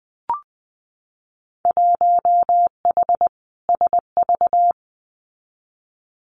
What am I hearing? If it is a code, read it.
Morse code, heard as a keyed tone, about 20 wpm, 700 Hz. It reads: 1H SV
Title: Winter Field Day Exchanges 20wpm